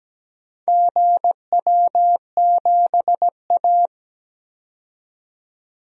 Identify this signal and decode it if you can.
Morse code, heard as a keyed tone, about 17 wpm, 700 Hz. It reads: GW7A
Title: DXCC-100 Call Signs - 4 Characters - 17wpm